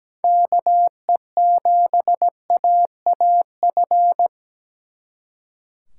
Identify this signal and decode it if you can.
Morse code, heard as a keyed tone, about 20 wpm, 700 Hz. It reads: KE7AAF
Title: Speed Racing - USA 2x3 Call Signs 17wpm